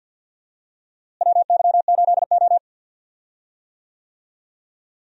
Morse code, heard as a keyed tone, about 51 wpm, 700 Hz. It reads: WY8O
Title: USA Call Signs - 4 Characters - Repeated 3x - 50wpm